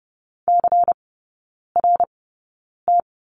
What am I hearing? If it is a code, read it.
Morse code, heard as a keyed tone, about 30 wpm, spaced as for 9 wpm, 700 Hz. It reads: CRT
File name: Lesson 19e - Letter w - Three Characters 30wpm